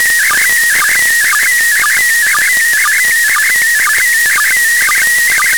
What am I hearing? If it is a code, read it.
SSTV (slow-scan television), PD 120.
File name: samples ogg / satnogs